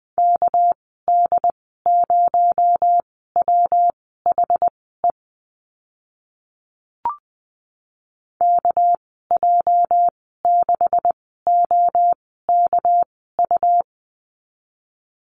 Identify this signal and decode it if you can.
Morse code, heard as a keyed tone, about 20 wpm, spaced as for 14 wpm, 700 Hz. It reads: KD0WHE KJ6OKU
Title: 219 Lesson 35g - Number 0 - Call Signs 20wpm